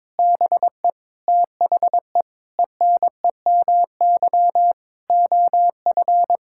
Morse code, heard as a keyed tone, about 21 wpm, 700 Hz. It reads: BE THE ENEMY OF
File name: Sets of 9 Words - 22wpm